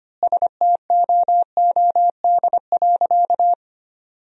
Morse code, heard as a keyed tone, about 25 wpm, 700 Hz. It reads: STOOD.
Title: 2x Word-Spacing - Sentences from Top 500 Words - 25wpm